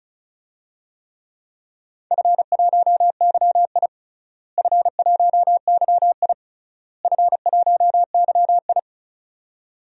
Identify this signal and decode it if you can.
Morse code, heard as a keyed tone, about 34 wpm, 700 Hz. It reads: F1YI F1YI F1YI
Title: DXCC-100 Call Signs - 4 Characters - Repeated 3x - 35wpm